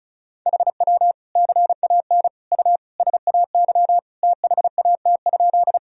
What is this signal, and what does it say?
Morse code, heard as a keyed tone, about 36 wpm, 700 Hz. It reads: HW CAN U SAY THAT?
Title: Sentences from Top 200 Words and Abbreviations 35wpm